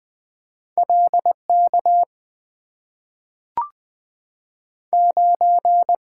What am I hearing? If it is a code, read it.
Morse code, heard as a keyed tone, about 20 wpm, 700 Hz. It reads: L K 9
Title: Two Letter-Number Combinations - 20wpm